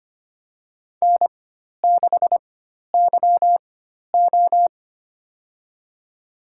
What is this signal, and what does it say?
Morse code, heard as a keyed tone, about 25 wpm, spaced as for 12 wpm, 700 Hz. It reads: N6YO
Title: USA Call Signs - 4 Characters - 4x Character Spacing - 25wpm